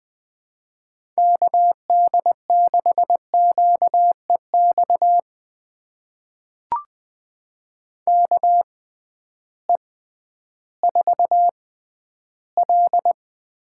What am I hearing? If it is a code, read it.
Morse code, heard as a keyed tone, about 20 wpm, spaced as for 12 wpm, 700 Hz. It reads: KD6QEX KE4L
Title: USA Call Signs - 6 Characters - 6x Character Spacing - 20wpm